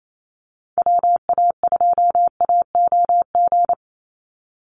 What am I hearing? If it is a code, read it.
Morse code, heard as a keyed tone, about 28 wpm, 700 Hz. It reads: WA2AOG
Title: Call Signs Repeated 2x - 28wpm